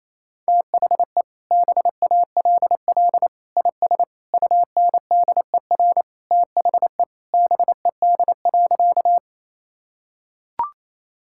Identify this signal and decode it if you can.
Morse code, heard as a keyed tone, about 28 wpm, 700 Hz. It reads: THE BALL IS UNDER THE BED.
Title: Sentences from Top 1000 Words - Review Part 6 - 28wpm